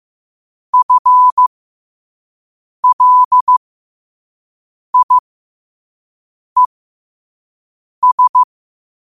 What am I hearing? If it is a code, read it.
Morse code, heard as a keyed tone, about 15 wpm, spaced as for 5 wpm, 1000 Hz. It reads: FLIES
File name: qotdfiles / 5wpm